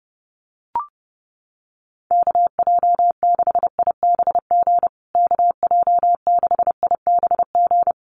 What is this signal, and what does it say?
Morse code, heard as a keyed tone, about 30 wpm, 700 Hz. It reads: KJ6IBG KJ6IBG
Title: USA Call Signs Repeated 2x 30wpm